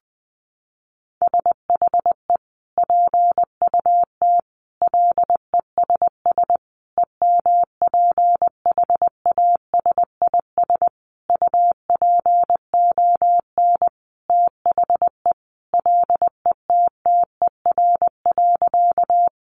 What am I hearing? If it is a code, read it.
Morse code, heard as a keyed tone, about 20 wpm, 700 Hz. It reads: SHE PUT LESS EMPHASIS UPON THE LETTER.
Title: Sentences from Top 1400 Words 20wpm